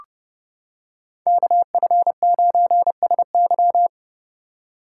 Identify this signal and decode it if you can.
Morse code, heard as a keyed tone, about 30 wpm, 700 Hz. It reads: KF9SY